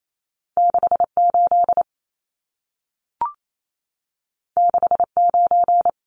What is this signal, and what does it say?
Morse code, heard as a keyed tone, about 28 wpm, 700 Hz. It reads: 68 69
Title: Numbers - 2-Digit 28wpm